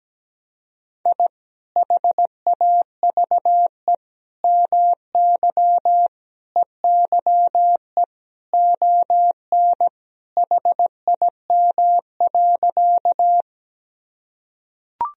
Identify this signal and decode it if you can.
Morse code, heard as a keyed tone, about 17 wpm, 700 Hz. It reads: I HAVE MY EYE ON HIM.
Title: Sentences from Top 1000 Words - Review Part 1 - 17wpm